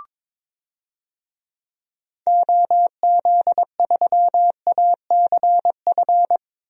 Morse code, heard as a keyed tone, about 22 wpm, 700 Hz. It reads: OZ3ACF
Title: DXCC-100 Call Signs - 6 Characters - 22wpm